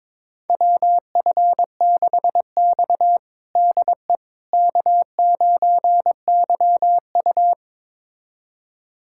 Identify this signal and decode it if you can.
Morse code, heard as a keyed tone, about 22 wpm, 700 Hz. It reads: WF6X DE K9YU
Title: USA Callsign DE Callsign - 4 Characters - 22wpm